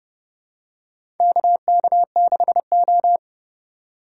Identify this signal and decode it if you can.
Morse code, heard as a keyed tone, about 30 wpm, 700 Hz. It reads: KK6O
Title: USA Call Signs - 4 Characters - 2x Character Spacing - 30wpm